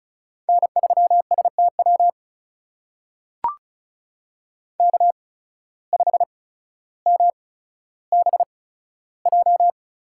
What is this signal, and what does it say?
Morse code, heard as a keyed tone, about 36 wpm, spaced as for 15 wpm, 700 Hz. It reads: N3STW K5MBJ
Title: USA Call Signs - 5 Characters - 8x Character Spacing - 35wpm